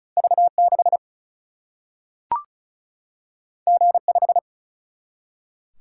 Morse code, heard as a keyed tone, about 36 wpm, 700 Hz. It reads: V 6 G 5
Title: Single-Letter Single-Number 35wpm